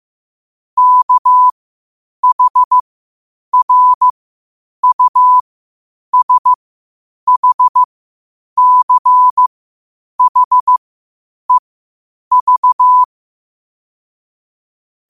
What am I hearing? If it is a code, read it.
Morse code, heard as a keyed tone, about 15 wpm, spaced as for 8 wpm, 1000 Hz. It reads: KHRUSHCHEV
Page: New quotes every day in morse code at 8 Words per minute.